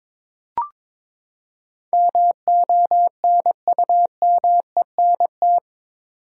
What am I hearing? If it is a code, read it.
Morse code, heard as a keyed tone, about 22 wpm, 700 Hz. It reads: MONUMENT
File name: practice / 22wpm / Eight Letter Words - 22wpm